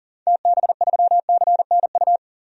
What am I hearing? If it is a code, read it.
Morse code, heard as a keyed tone, about 40 wpm, 700 Hz. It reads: TB3CNU
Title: DXCC-100 Call Signs - 6 Characters - 40wpm